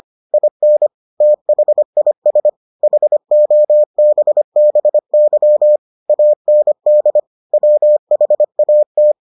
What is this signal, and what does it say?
Morse code, heard as a keyed tone, about 25 wpm, 600 Hz. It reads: IN THIS HOBBY AND WHAT